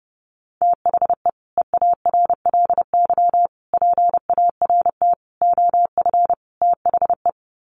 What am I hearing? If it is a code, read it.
Morse code, heard as a keyed tone, about 28 wpm, 700 Hz. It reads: THE EARLY PART OF THE
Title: Sets of 5 Words Encore2 30wpm